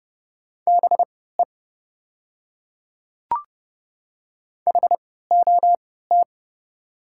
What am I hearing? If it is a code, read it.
Morse code, heard as a keyed tone, about 31 wpm, spaced as for 17 wpm, 700 Hz. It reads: BE HOT
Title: 3x Character Spacing - Top 100 Words - 30wpm